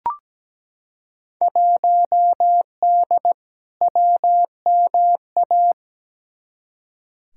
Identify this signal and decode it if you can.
Morse code, heard as a keyed tone, about 17 wpm, 700 Hz. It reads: 1D WMA
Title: Field Day Exchanges - 17wpm